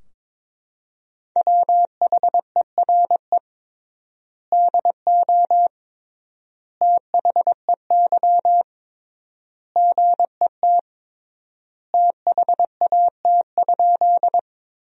Morse code, heard as a keyed tone, about 22 wpm, 700 Hz. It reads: WHERE DO THEY GET THAT?
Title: 3x Word-Spacing - Sentences from Top 200 Words - 22wpm